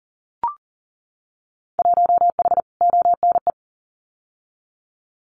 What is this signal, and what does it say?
Morse code, heard as a keyed tone, about 38 wpm, 700 Hz. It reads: 1H ONE
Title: Winter Field Day Exchanges Encore 40wpm